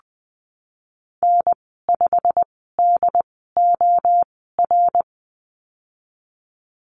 Morse code, heard as a keyed tone, about 20 wpm, spaced as for 15 wpm, 700 Hz. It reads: N5DOR
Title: 122 Lesson 20g - Number 3 - Call Signs 20wpm